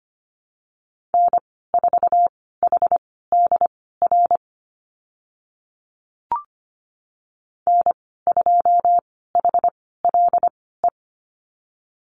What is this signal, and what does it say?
Morse code, heard as a keyed tone, about 25 wpm, spaced as for 16 wpm, 700 Hz. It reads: N4HDR N2HLE
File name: Lesson 15g - Number 5 - Call Signs 25wpm